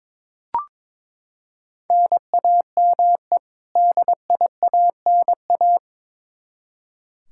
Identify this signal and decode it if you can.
Morse code, heard as a keyed tone, about 21 wpm, 700 Hz. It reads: NAME DIANA
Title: Practice QSO - Exchange 1 - Beginning - 22wpm